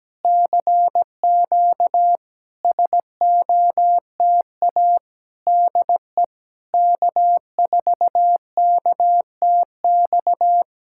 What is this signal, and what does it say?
Morse code, heard as a keyed tone, about 17 wpm, 690 Hz. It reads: CQ SOTA DE K4KTX
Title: SOTA QSO Practice 17wpm